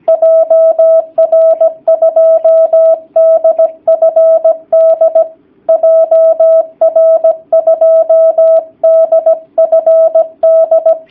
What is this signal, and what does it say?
Morse code, heard as a keyed tone, about 17 wpm, 650 Hz. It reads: JR2DFD JR2DFD
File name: jh8cbh / 2019cw / sp16- w